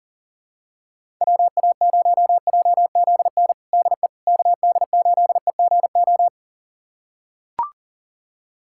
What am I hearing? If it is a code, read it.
Morse code, heard as a keyed tone, about 37 wpm, 700 Hz. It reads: WA0JZN DE KD8EGO
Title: USA Callsign DE Callsign 40wpm